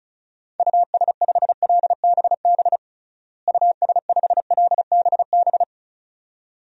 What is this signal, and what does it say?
Morse code, heard as a keyed tone, about 34 wpm, 700 Hz. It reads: US5LBB US5LBB
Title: DXCC-100 Call Signs - 6 Characters - Repeated 2x - 35wpm